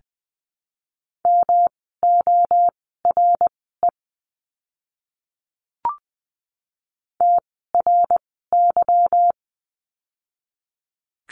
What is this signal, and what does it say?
Morse code, heard as a keyed tone, about 20 wpm, spaced as for 15 wpm, 700 Hz. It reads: MORE TRY
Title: 160 Lesson 26f - Letter g - Words 20wpm